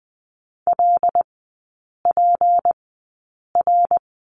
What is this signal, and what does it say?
Morse code, heard as a keyed tone, about 20 wpm, spaced as for 8 wpm, 700 Hz. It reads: LPR
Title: 198 Lesson 32e - Letter k - Three Characters 20wpm